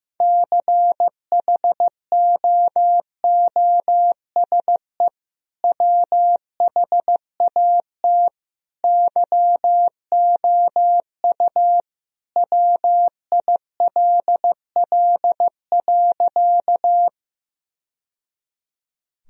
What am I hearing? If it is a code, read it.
Morse code, heard as a keyed tone, about 15 wpm, 700 Hz. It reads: CHOOSE WHAT YOU WILL.
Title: Sentences from Top 1000 Words - Review Part 4 - 15wpm